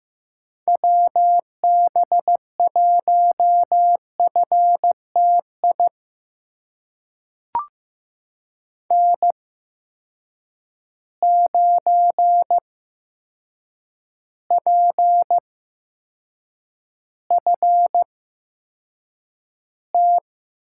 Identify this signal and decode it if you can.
Morse code, heard as a keyed tone, about 15 wpm, spaced as for 7 wpm, 700 Hz. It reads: WB1FTI N9PFT
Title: USA Call Signs - 8x Character Spacing - 15wpm